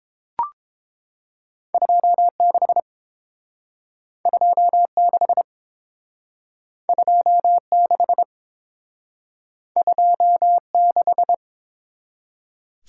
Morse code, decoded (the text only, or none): 26 26 26 26